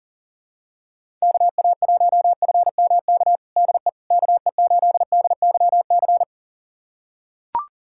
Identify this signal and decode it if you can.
Morse code, heard as a keyed tone, about 40 wpm, 700 Hz. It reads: KA1FMK DE KE8DYC
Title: USA Callsign DE Callsign - 6 Characters - 40wpm